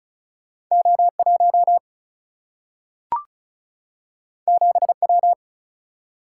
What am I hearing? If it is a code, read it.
Morse code, heard as a keyed tone, about 36 wpm, 700 Hz. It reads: O1 7W
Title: Single-Letter Single-Number 35wpm